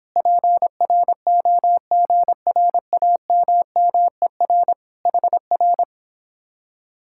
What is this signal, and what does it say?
Morse code, heard as a keyed tone, about 26 wpm, 700 Hz. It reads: PROGRAMMER HR